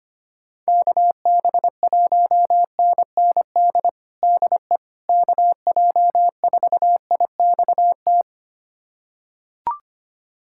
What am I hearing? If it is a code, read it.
Morse code, heard as a keyed tone, about 25 wpm, 700 Hz. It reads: KB1NND DE KJ4IXT
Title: USA Callsign DE Callsign - 6 Characters - 25wpm